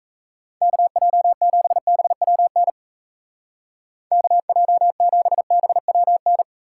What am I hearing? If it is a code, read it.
Morse code, heard as a keyed tone, about 34 wpm, 700 Hz. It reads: KJ7BWN KJ7BWN
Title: Speed Racing - USA 2x3 Call Signs 28wpm